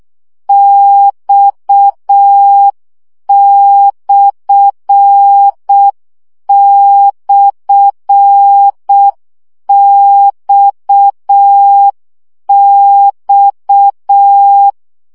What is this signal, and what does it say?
Morse code, heard as a keyed tone, about 6 wpm, 800 Hz. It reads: X//XX
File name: CW gyakorlas x per 5 baud